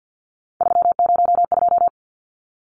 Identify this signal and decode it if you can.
Morse code, heard as a keyed tone, about 50 wpm, 700 Hz. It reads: 302
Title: Numbers - 3-Digit 50wpm